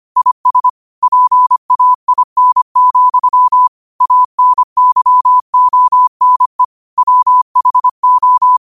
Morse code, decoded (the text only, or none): IS PAIN, ANYONE WHO